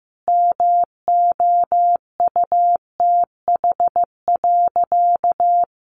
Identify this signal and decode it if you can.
Morse code, heard as a keyed tone, about 14 wpm, 700 Hz. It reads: MOUTH.
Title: Sentences from Top 1500 Words 15wpm